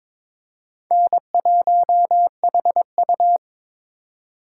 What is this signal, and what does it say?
Morse code, heard as a keyed tone, about 23 wpm, 700 Hz. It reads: N1HU
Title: USA Call Signs - 4 Characters - 4x Character Spacing - 22wpm